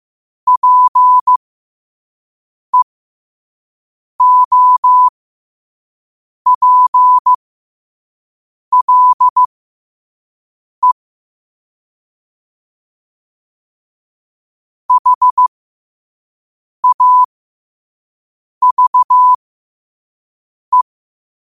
Morse code, heard as a keyed tone, about 15 wpm, spaced as for 5 wpm, 1000 Hz. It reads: PEOPLE HAVE
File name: qotdfiles / 5wpm